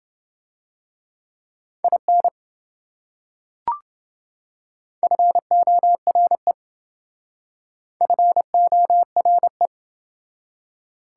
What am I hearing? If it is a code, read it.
Morse code, heard as a keyed tone, about 28 wpm, 700 Hz. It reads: IN FORE FORE